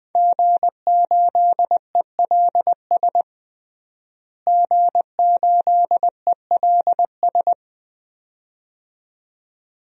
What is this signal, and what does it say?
Morse code, heard as a keyed tone, about 20 wpm, 700 Hz. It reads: G8ELS G8ELS
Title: DXCC-100 Call Signs - 5 Characters - Repeated 2x - 20wpm